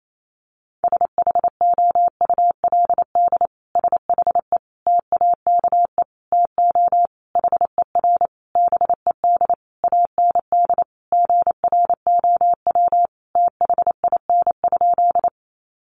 Morse code, heard as a keyed tone, about 28 wpm, 700 Hz. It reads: SHOULD SHE TAKE TO HER BED AND GROW THIN?
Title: Sentences from Top 1200 Words 28wpm